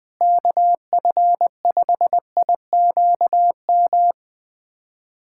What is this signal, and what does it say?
Morse code, heard as a keyed tone, about 20 wpm, 700 Hz. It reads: KF5IQM
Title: USA Callsign DE Callsign 20wpm